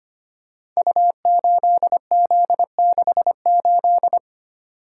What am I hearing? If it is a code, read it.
Morse code, heard as a keyed tone, about 25 wpm, 700 Hz. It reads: U8Z68
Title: Code Groups - 5 Letter:Numbers - 25wpm